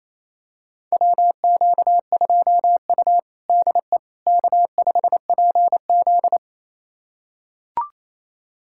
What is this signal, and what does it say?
Morse code, heard as a keyed tone, about 28 wpm, 700 Hz. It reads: WQ2U DE K5PZ
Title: USA Callsign DE Callsign - 4 Characters - 28wpm